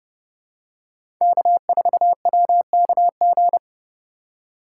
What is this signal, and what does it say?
Morse code, heard as a keyed tone, about 30 wpm, 700 Hz. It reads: K4WKG
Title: USA Call Signs - 5 Characters - 3x Character Spacing - 30wpm